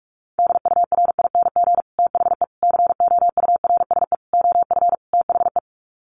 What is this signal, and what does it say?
Morse code, heard as a keyed tone, about 44 wpm, 700 Hz. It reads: DURING THE COURSE OF THE
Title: Sets of 5 Words Encore 45wpm